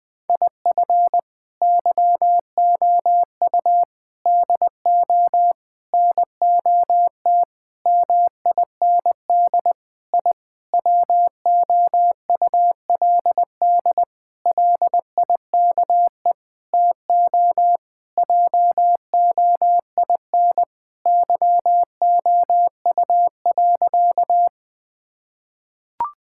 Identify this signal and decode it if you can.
Morse code, heard as a keyed tone, about 20 wpm, 700 Hz. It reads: IF YOU DO NOT MIND I WOULD LIKE TO JOIN YOU.
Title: Sentences with 5-Word Phrases - 20wpm